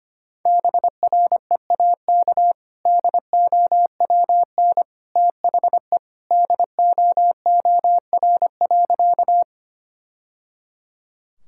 Morse code, heard as a keyed tone, about 25 wpm, 700 Hz. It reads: BREAK DOWN THE DOOR.
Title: Sentences from Top 1000 Words - Review Part 6 - 25wpm